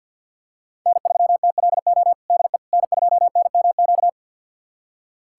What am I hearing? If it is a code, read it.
Morse code, heard as a keyed tone, about 46 wpm, 700 Hz. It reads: N3TLO DE N2NMQ
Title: USA Callsign DE Callsign - 5 Characters - 50wpm